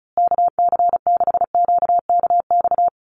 Morse code, heard as a keyed tone, about 34 wpm, 700 Hz. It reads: KC6QKX
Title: USA Call Signs Repeated 3x 35wpm